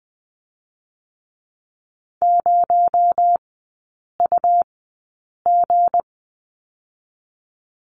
Morse code, heard as a keyed tone, about 20 wpm, 700 Hz. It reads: 0 U G
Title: 218 Lesson 35e - Number 0 - Three Characters 20wpm